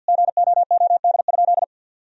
Morse code, heard as a keyed tone, about 50 wpm, 700 Hz. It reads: GOOD?
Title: Sentences from Top 1000 Words - Review Part 3 - 50wpm